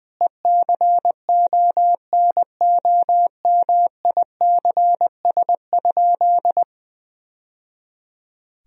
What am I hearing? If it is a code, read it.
Morse code, heard as a keyed tone, about 21 wpm, 700 Hz. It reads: ECONOMICS?
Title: Sentences from Top 1000 Words - Review Part 3 - 20wpm